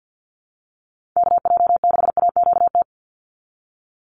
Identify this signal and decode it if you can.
Morse code, heard as a keyed tone, about 48 wpm, 700 Hz. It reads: KJ6AQT
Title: USA Call Signs 50wpm